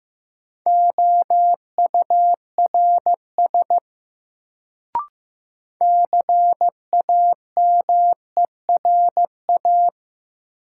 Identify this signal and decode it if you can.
Morse code, heard as a keyed tone, about 15 wpm, 700 Hz. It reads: OURS CAMERA
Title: Sentences from Top 1000 Words - Review Part 5 - New Word Drill - 15wpm